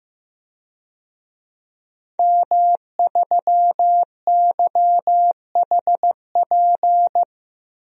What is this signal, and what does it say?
Morse code, heard as a keyed tone, about 16 wpm, 700 Hz. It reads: M3YHP
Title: DXCC-100 Call Signs - 5 Characters - Repeated 2x - 15wpm